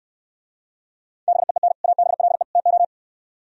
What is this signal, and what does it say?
Morse code, heard as a keyed tone, about 67 wpm, 700 Hz. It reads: BEEN ABLE TO